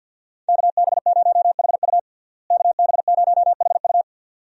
Morse code, heard as a keyed tone, about 52 wpm, 700 Hz. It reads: KB0HU KB0HU
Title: USA Call Signs - 5 Characters - Repeated 2x - 50wpm